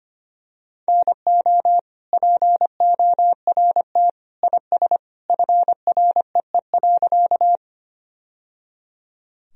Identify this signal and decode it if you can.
Morse code, heard as a keyed tone, about 25 wpm, 700 Hz. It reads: NO PORT IS FREE.
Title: Sentences from Top 1000 Words - Review Part 1 - 25wpm